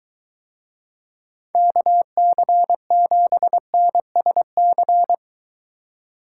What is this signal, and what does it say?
Morse code, heard as a keyed tone, about 23 wpm, 700 Hz. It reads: KC7NSC